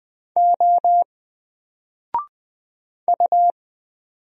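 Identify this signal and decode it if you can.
Morse code, heard as a keyed tone, about 20 wpm, 700 Hz. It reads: O U
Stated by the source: Single letter and number Morse drill at 20 wpm, from N4LCD.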